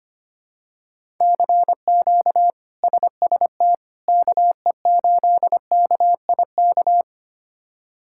Morse code, heard as a keyed tone, about 25 wpm, 700 Hz. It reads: CQ SST KE8KIK
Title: K1USN SST QSO Practice - 25wpm